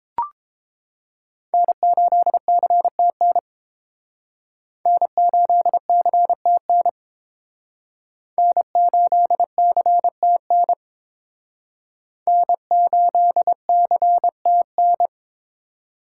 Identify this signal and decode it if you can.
Morse code, heard as a keyed tone, about 28 wpm, 700 Hz. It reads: N8CTN N8CTN N8CTN N8CTN